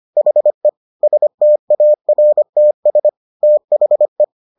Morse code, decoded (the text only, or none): HE STARTS THE